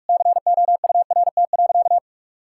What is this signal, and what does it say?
Morse code, heard as a keyed tone, about 46 wpm, 700 Hz. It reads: COURT.
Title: Sentences from Top 1600 Words 45wpm